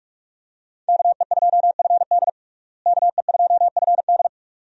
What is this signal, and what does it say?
Morse code, heard as a keyed tone, about 44 wpm, 700 Hz. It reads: KE2FD KE2FD
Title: USA Call Signs - 5 Characters - Repeated 2x - 45wpm